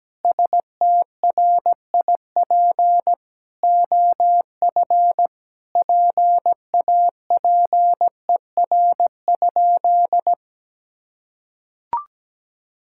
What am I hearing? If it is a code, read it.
Morse code, heard as a keyed tone, about 17 wpm, 700 Hz. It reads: STRIP OF PAPER?
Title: Sentences from Top 1600 Words 17wpm